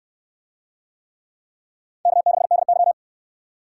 Morse code, heard as a keyed tone, about 67 wpm, 700 Hz. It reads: K6DY